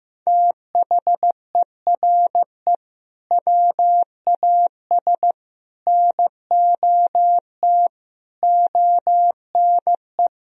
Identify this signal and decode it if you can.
Morse code, heard as a keyed tone, about 15 wpm, 700 Hz. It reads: THERE WAS NOT ONE
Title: Sentences from Top 1000 Words - Review Part 2 - 15wpm